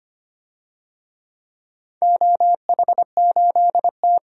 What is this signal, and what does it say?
Morse code, heard as a keyed tone, about 25 wpm, 700 Hz. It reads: OH8T
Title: DXCC-100 Call Signs - 4 Characters - 25wpm